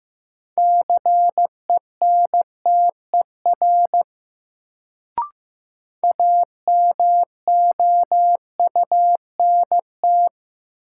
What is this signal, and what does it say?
Morse code, heard as a keyed tone, about 15 wpm, 700 Hz. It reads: CENTER AMOUNT
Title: Sets of 3 Words - New Word Drill - 15wpm